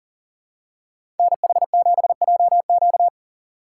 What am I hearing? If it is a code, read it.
Morse code, heard as a keyed tone, about 40 wpm, 700 Hz. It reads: NH7JQ
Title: USA Call Signs - 5 Characters - 4x Character Spacing - 40wpm